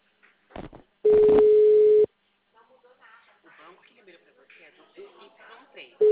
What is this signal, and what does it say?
Ringback tone (Europe, 425 Hz), with speech.